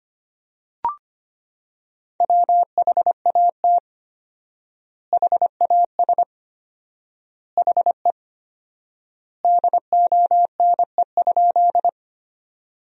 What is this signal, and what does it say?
Morse code, heard as a keyed tone, about 25 wpm, 700 Hz. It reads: WHAT HAS HE DONE?
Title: 4x Word-Spacing - Sentences from Top 500 Words - 25wpm